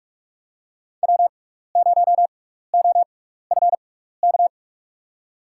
Morse code, heard as a keyed tone, about 47 wpm, spaced as for 16 wpm, 700 Hz. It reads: W0OFK
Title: USA Call Signs - 5 Characters - 6x Character Spacing - 45wpm